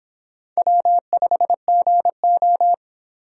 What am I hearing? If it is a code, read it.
Morse code, heard as a keyed tone, about 25 wpm, 700 Hz. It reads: W5GO